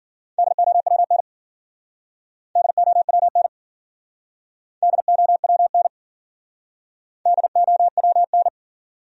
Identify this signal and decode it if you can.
Morse code, heard as a keyed tone, about 53 wpm, 700 Hz. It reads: DOWN DOWN DOWN DOWN